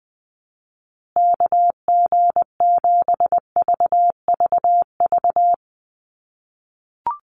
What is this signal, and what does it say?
Morse code, heard as a keyed tone, about 20 wpm, 700 Hz. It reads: KG7VVV
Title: USA Call Signs Repeated 3x 20wpm